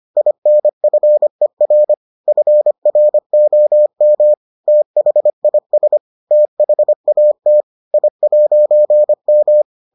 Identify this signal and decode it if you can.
Morse code, heard as a keyed tone, about 25 wpm, 600 Hz. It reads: INFER FROM THIS THAT I'M